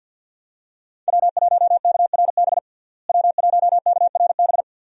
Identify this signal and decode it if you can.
Morse code, heard as a keyed tone, about 49 wpm, 700 Hz. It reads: W1KRB W1KRB
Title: USA Call Signs - 5 Characters - Repeated 2x - 50wpm